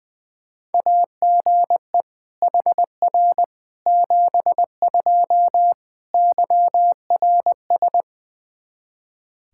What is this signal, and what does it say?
Morse code, heard as a keyed tone, about 21 wpm, 700 Hz. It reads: AGE HR 72 YRS
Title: Practice QSO - Exchange 3 - More Info - 20wpm